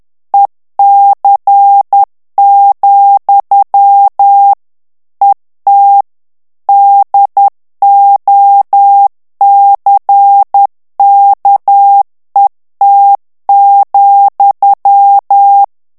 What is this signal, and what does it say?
Morse code, heard as a keyed tone, about 10 wpm, 800 Hz. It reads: EC, ET DOCKET,
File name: ANGLAIS texte clair 3 vitesse 10 mots minute